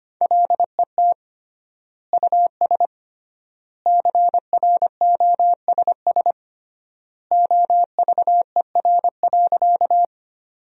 Morse code, heard as a keyed tone, about 25 wpm, 700 Hz. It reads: LET US CROSS OVER.
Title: 3x Word-Spacing - Sentences from Top 300 Words - 25wpm